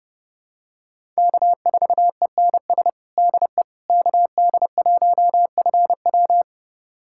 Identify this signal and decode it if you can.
Morse code, heard as a keyed tone, about 31 wpm, 700 Hz. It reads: K4ENS DE KD1FW
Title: USA Callsign DE Callsign 30wpm